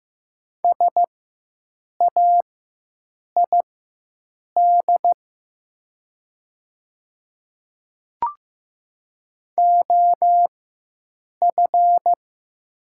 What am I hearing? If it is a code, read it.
Morse code, heard as a keyed tone, about 15 wpm, spaced as for 7 wpm, 700 Hz. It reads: SAID OF
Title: 4x Character Spacing - Top 100 Words - 15wpm